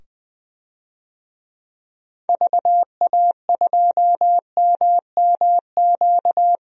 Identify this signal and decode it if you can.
Morse code, heard as a keyed tone, about 21 wpm, 700 Hz. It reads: VA2MMQ
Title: DXCC-100 Call Signs - 6 Characters - 20wpm